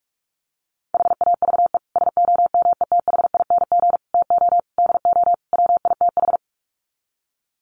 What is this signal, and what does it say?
Morse code, heard as a keyed tone, about 44 wpm, 700 Hz. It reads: HAVE SOMETHING TO DO WITH
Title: Sets of 5 Words Encore 45wpm